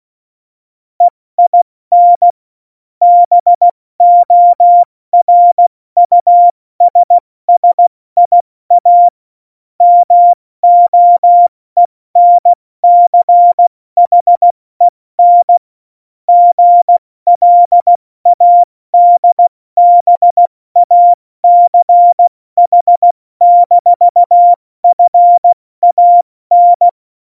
Morse code, heard as a keyed tone, about 16 wpm, spaced as for 14 wpm, 700 Hz. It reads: EIN BORUSSIA MOENCHEN GLADBACH-FAN
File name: Morsekurs / Witze / Witz 22. Dezember 2020, 70 F80